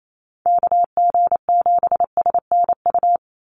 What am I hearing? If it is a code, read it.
Morse code, heard as a keyed tone, about 28 wpm, 700 Hz. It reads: KG7SNU
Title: USA Call Signs 28wpm